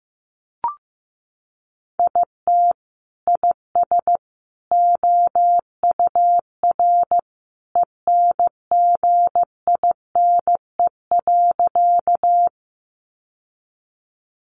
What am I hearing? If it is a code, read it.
Morse code, heard as a keyed tone, about 15 wpm, 700 Hz. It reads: IT IS OUR ENGINE.
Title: Sentences from Top 1500 Words 15wpm